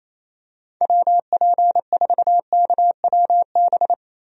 Morse code, heard as a keyed tone, about 28 wpm, 700 Hz. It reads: WP4KWB
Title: USA Call Signs - 6 Characters - Repeated 3x - 28wpm